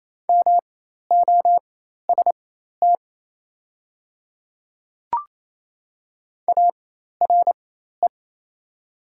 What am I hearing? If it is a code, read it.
Morse code, heard as a keyed tone, about 27 wpm, spaced as for 13 wpm, 690 Hz. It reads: MOST ARE
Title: 4x Character Spacing - Top 100 Words - 28wpm